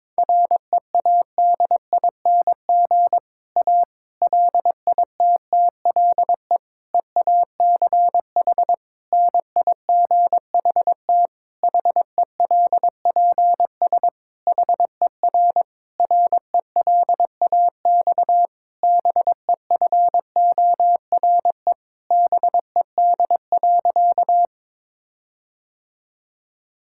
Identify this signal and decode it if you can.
Morse code, heard as a keyed tone, about 22 wpm, 700 Hz. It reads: READING A LITTLE EACH NIGHT HELPS HER RELAX BEFORE BED.
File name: Difficult Sentences with Hints - 22wpm